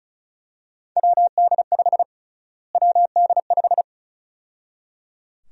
Morse code, heard as a keyed tone, about 36 wpm, 700 Hz. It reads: WD5 WD5
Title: USA Call Sign Prefixes - Repeated 2x - 35wpm